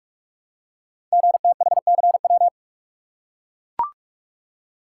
Morse code, heard as a keyed tone, about 44 wpm, 690 Hz. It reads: GTHCW
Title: Code Groups - 5 Letters - 45wpm